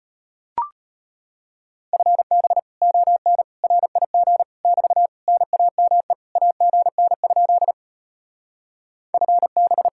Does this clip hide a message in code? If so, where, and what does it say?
Yes: Morse code from 1.9 to 9.9 s, heard as a keyed tone, about 33 wpm, 700 Hz. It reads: FB ON RIG <BT> NAME AGN? FB